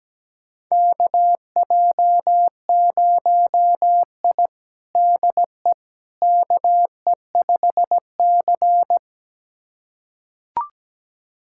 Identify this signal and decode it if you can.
Morse code, heard as a keyed tone, about 17 wpm, 700 Hz. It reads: KJ0I DE KE5C
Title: USA Callsign DE Callsign - 4 Characters - 17wpm